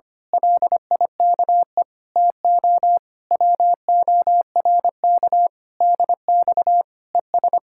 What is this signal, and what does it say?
Morse code, heard as a keyed tone, about 25 wpm, 700 Hz. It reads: LIKE TO WORK DX ES
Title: Speed-Racing Common QSO Phrases - 25wpm